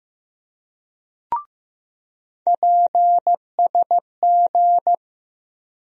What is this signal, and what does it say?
Morse code, heard as a keyed tone, about 15 wpm, 700 Hz. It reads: PSG
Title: USA Call Sign Postfixes - 3 Letters 15wpm